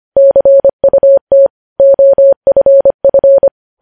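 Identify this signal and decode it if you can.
Morse code, heard as a keyed tone, about 25 wpm, 550 Hz. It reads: CUT OFF
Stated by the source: News headlines sent in Morse at 25.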